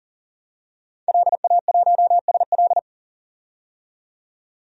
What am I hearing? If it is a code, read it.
Morse code, heard as a keyed tone, about 40 wpm, 700 Hz. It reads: LA1SL
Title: DXCC-100 Call Signs - 5 Characters - 40wpm